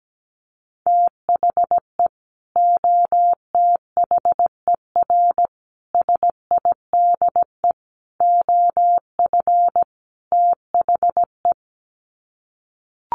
Morse code, heard as a keyed tone, about 17 wpm, 700 Hz. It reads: THE OTHER SIDE OF THE
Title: Sets of 5 Words 17wpm